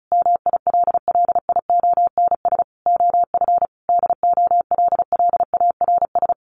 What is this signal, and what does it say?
Morse code, heard as a keyed tone, about 34 wpm, 700 Hz. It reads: MILLIONS OF DOLLARS
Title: Sets of 5 Words 35wpm